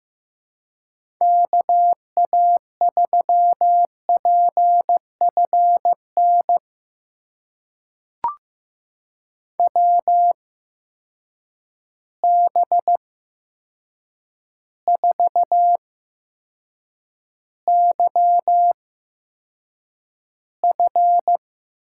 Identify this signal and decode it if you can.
Morse code, heard as a keyed tone, about 15 wpm, spaced as for 7 wpm, 700 Hz. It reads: KA3PFN WB4YF
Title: USA Call Signs - 6 Characters - 8x Character Spacing - 15wpm